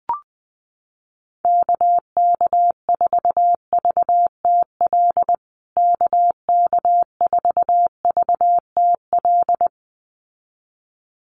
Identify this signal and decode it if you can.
Morse code, heard as a keyed tone, about 20 wpm, 700 Hz. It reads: KK4VTL KK4VTL
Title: USA Call Signs Repeated 2x 20wpm